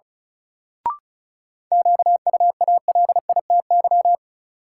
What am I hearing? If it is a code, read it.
Morse code, heard as a keyed tone, about 36 wpm, 700 Hz. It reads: QUALITY
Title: Sentences from Top 1500 Words - New Word Drill Alt - 35wpm